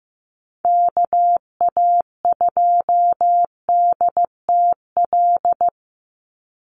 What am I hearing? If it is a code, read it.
Morse code, heard as a keyed tone, about 15 wpm, 700 Hz. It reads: KA2DTL
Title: USA Call Signs 15wpm